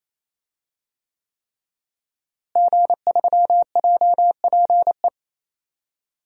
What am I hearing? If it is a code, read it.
Morse code, heard as a keyed tone, about 28 wpm, 690 Hz. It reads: G3JPE
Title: DXCC-100 Call Signs - 5 Characters - Repeated 3x - 28wpm